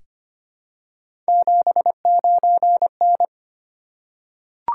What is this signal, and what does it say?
Morse code, heard as a keyed tone, about 25 wpm, 700 Hz. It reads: 79N
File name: Three Letter-Number Combinations - 25wpm